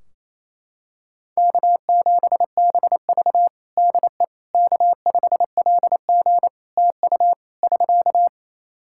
Morse code, heard as a keyed tone, about 27 wpm, 700 Hz. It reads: K7BV DE K5LG TU <SK>
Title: K1USN SST QSO Practice 28wpm